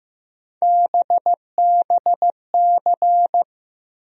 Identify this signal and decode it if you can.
Morse code, heard as a keyed tone, about 15 wpm, 700 Hz. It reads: BBC
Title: USA Call Sign Postfixes - 3 Letters 15wpm